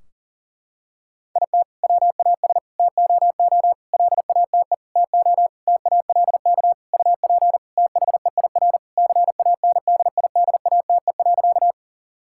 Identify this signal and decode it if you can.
Morse code, heard as a keyed tone, about 40 wpm, 700 Hz. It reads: IT WAS TOO LATE TO TALK UP THEIR CANDIDATE.
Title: Sentences from Top 1600 Words 40wpm